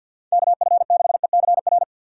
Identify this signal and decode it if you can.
Morse code, heard as a keyed tone, about 47 wpm, 700 Hz. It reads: KF6ECR
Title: USA Call Signs - 6 Characters - 4x Character Spacing - 50wpm